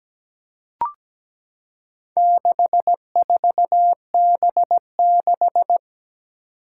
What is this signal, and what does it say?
Morse code, heard as a keyed tone, about 17 wpm, 700 Hz. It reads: 64B6
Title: Four Challenging Letter-Number Combinations 17wpm